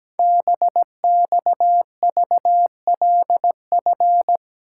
Morse code, heard as a keyed tone, about 17 wpm, 700 Hz. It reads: BXVLF
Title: Code Groups - 5 Letters - 17wpm